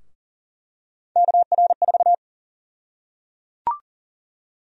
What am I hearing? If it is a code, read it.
Morse code, heard as a keyed tone, about 38 wpm, 700 Hz. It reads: KR4
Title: USA Call Sign Prefixes - Repeated 2x - 40wpm